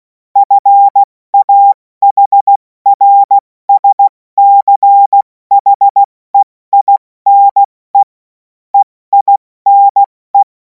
Morse code, heard as a keyed tone, about 16 wpm, spaced as for 14 wpm, 800 Hz. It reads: FAHRSCHEINE EINE
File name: Morsekurs / Witze / Witz 03. Februar 2026 70 F80